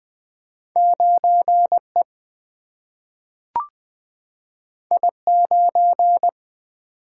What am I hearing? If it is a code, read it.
Morse code, heard as a keyed tone, about 20 wpm, 700 Hz. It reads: 9E I9
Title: Two Letter-Number Combinations - 20wpm